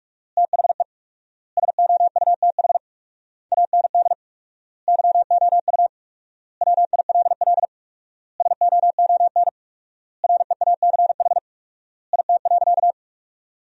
Morse code, heard as a keyed tone, about 46 wpm, 700 Hz. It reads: THE SOUTH AND YOU WILL SOON REACH IT.
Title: 4x Word-Spacing - Sentences from Top 400 Words - 45wpm